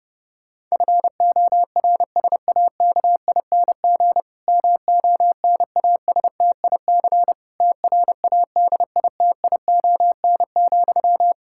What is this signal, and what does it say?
Morse code, heard as a keyed tone, about 30 wpm, 700 Hz. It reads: FORSAKING MONASTIC TRADITION,
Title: Pangram Challenge v2 - 30wpm